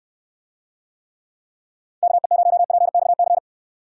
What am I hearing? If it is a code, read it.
Morse code, heard as a keyed tone, about 69 wpm, 700 Hz. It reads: KE0PXC